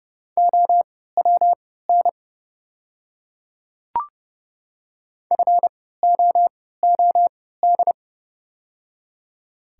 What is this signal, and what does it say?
Morse code, heard as a keyed tone, about 32 wpm, spaced as for 17 wpm, 690 Hz. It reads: OWN FOOD
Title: 3x Character Spacing - Top 500 Words - 30wpm